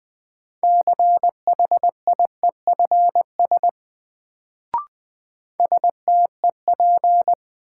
Morse code, heard as a keyed tone, about 20 wpm, 700 Hz. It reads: CHIEFS STEP
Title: Sets of 5 Words Encore - New Word Drill - 20wpm